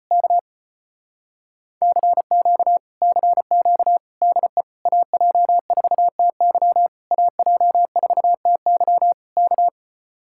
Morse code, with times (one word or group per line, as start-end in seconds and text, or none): 0.1-0.4 K
1.8-2.8 CQ
3.0-4.0 CQ
4.2-4.6 DE
4.9-6.9 AJ4TY
7.1-9.1 AJ4TY
9.4-9.7 K